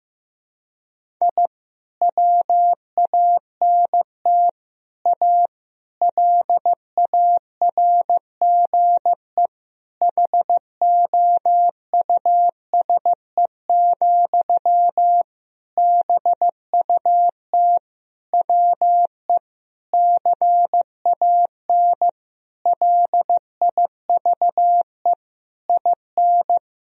Morse code, heard as a keyed tone, about 16 wpm, 700 Hz. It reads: I WANT A LARGE HOUSE, BUT WE CAN LIVE IN
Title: Sentences from Top 1000 Words - Review Part 1 - 15wpm